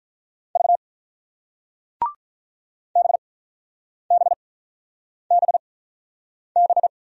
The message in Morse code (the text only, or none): V BBBB